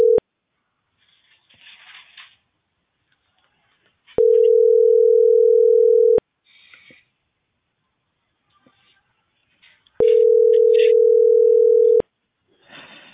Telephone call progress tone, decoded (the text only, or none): ringback tone